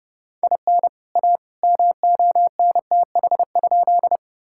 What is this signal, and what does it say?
Morse code, heard as a keyed tone, about 31 wpm, 700 Hz. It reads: IN A MONTH?
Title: Sentences from Top 1000 Words - Review Part 5 - 30wpm